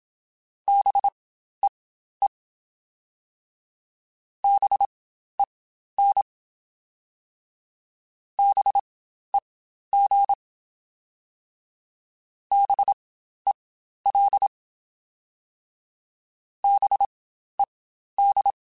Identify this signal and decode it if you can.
Morse code, heard as a keyed tone, about 27 wpm, spaced as for 12 wpm, 800 Hz. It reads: BEE BEN BEG BEL BED
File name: Spaced Alphas / Spaced Alphas mp3s / Bs sp @25